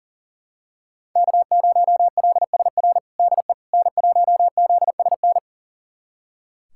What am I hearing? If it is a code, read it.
Morse code, heard as a keyed tone, about 40 wpm, 700 Hz. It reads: K0LSR DE N1ZSN
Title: USA Callsign DE Callsign - 5 Characters - 40wpm